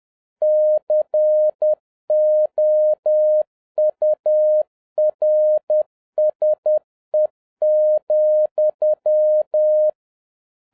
Morse code, heard as a keyed tone, about 10 wpm, 610 Hz. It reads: COURSE,